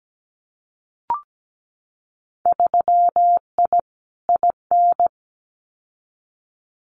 Morse code, heard as a keyed tone, about 17 wpm, 700 Hz. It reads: 3I IN
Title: Winter Field Day Exchanges 17wpm